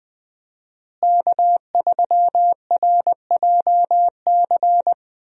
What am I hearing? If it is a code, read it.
Morse code, heard as a keyed tone, about 20 wpm, 700 Hz. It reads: K3RJC
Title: USA Call Signs - 5 Characters - Repeated 2x - 20wpm